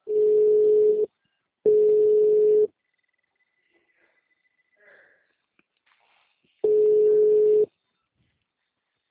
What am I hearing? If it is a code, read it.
Ringback tone (Europe, 425 Hz).
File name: out- -2028- - - .36161